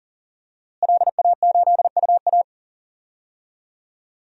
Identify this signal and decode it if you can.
Morse code, heard as a keyed tone, about 40 wpm, 700 Hz. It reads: LA8UA
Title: DXCC-100 Call Signs - 5 Characters - Repeated 3x - 40wpm